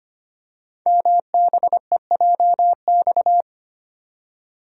Morse code, heard as a keyed tone, about 25 wpm, 700 Hz. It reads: MBEJX
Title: Code Groups - 5 Letters - 25wpm